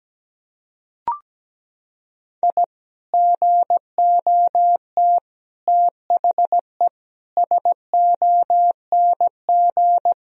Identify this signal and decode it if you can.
Morse code, heard as a keyed tone, about 17 wpm, 700 Hz. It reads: I GOT THE SONG
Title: Sentences from Top 1000 Words - Review Part 2 - 17wpm